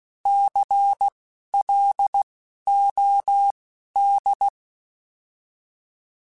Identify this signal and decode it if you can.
Morse code, heard as a keyed tone, about 16 wpm, spaced as for 11 wpm, 800 Hz. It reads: CLOD
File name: Alphabetical / Alphabetical mp3s / Cs @10